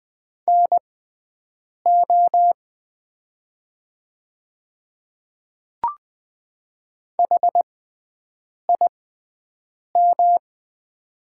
Morse code, heard as a keyed tone, about 20 wpm, spaced as for 7 wpm, 700 Hz. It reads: NO HIM
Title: 6x Character Spacing - Top 300 Words - 20wpm